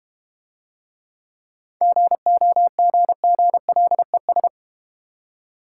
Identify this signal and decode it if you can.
Morse code, heard as a keyed tone, about 30 wpm, 700 Hz. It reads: GOGGLES
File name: Easy Words - Speed-Racing with Hints - 25wpm